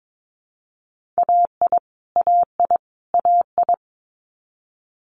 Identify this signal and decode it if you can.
Morse code, heard as a keyed tone, about 22 wpm, 700 Hz. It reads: AI AI AI
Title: Multi-letter Phonemes Repeated 3x 22wpm